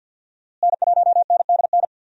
Morse code, heard as a keyed tone, about 50 wpm, 700 Hz. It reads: N1NDN
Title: USA Call Signs - 5 Characters - Repeated 2x - 50wpm